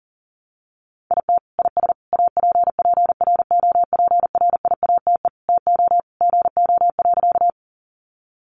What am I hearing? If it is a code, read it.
Morse code, heard as a keyed tone, about 40 wpm, 700 Hz. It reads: IT IS APPROPRIATE TO GO.
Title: Sentences from Top 900 Words 40wpm